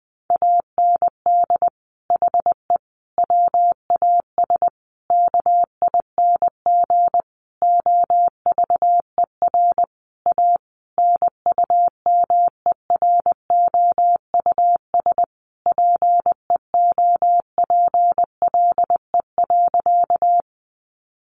Morse code, heard as a keyed tone, about 20 wpm, 700 Hz. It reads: AND HE WAS KING OVER A NUMEROUS PEOPLE.
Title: Sentences from Top 1400 Words 20wpm